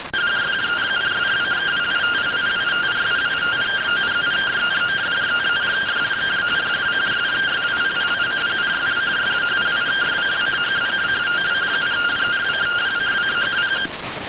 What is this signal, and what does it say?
Two-tone teleprinter signal, about 170 Hz shift.